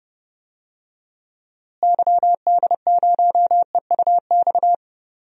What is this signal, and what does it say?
Morse code, heard as a keyed tone, about 30 wpm, 700 Hz. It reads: YD0EUX
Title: DXCC-100 Call Signs - 6 Characters - Repeated 3x - 30wpm